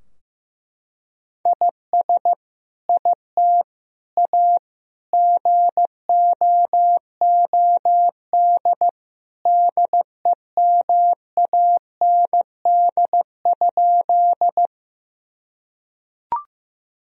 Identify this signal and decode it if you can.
Morse code, heard as a keyed tone, about 16 wpm, 700 Hz. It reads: IS IT A GOOD DEMAND?
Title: Sentences from Top 1000 Words - Review Part 4 - 15wpm